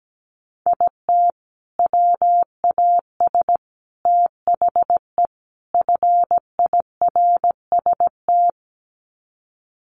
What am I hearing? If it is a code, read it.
Morse code, heard as a keyed tone, about 17 wpm, 700 Hz. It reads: IT WAS THE FIRST
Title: Sets of 4 Words 17wpm